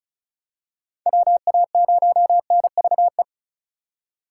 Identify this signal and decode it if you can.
Morse code, heard as a keyed tone, about 36 wpm, 700 Hz. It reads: WA0NVE
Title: USA Call Signs - 6 Characters - 2x Character Spacing - 35wpm